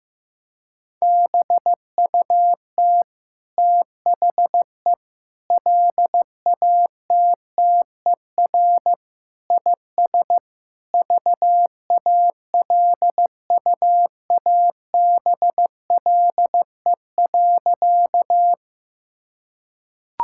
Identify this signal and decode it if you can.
Morse code, heard as a keyed tone, about 15 wpm, 700 Hz. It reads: BUT THE LATTER IS VALUABLE.
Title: Sentences from Top 1600 Words 15wpm